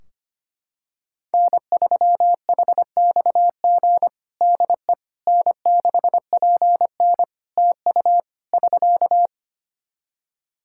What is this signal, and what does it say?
Morse code, heard as a keyed tone, about 25 wpm, 700 Hz. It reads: N3HXG DE N6PN TU <SK>
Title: K1USN SST QSO Practice 25wpm